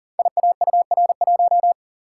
Morse code, heard as a keyed tone, about 38 wpm, 690 Hz. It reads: IAUR1
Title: Code Groups - 5 Letter:Numbers - 40wpm